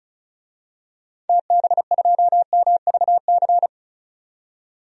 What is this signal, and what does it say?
Morse code, heard as a keyed tone, about 34 wpm, 700 Hz. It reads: TB2MVC
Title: DXCC-100 Call Signs - 6 Characters - 35wpm